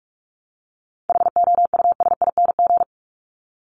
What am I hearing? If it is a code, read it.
Morse code, heard as a keyed tone, about 44 wpm, 700 Hz. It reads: HOUSING
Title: ING Suffix - Top 100 Words with Base Words 45wpm